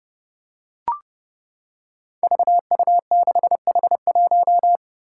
Morse code, heard as a keyed tone, about 30 wpm, 700 Hz. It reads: VU6H1
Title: Five Challenging Letter-Number Combinations 30wpm